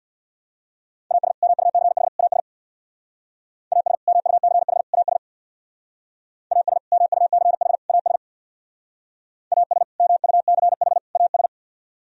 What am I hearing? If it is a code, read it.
Morse code, heard as a keyed tone, about 60 wpm, 700 Hz. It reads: AS MUCH AS AS MUCH AS AS MUCH AS AS MUCH AS
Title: Speed-Racing - Sets of 3 Words 50wpm